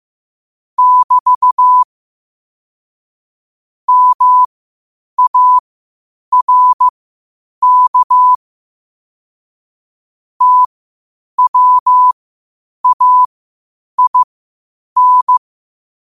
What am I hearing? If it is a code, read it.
Morse code, heard as a keyed tone, about 15 wpm, spaced as for 8 wpm, 1000 Hz. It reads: <BT> MARK TWAIN
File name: qotdfiles / 8wpm